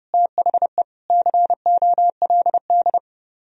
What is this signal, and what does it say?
Morse code, heard as a keyed tone, about 39 wpm, 700 Hz. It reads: THE COLD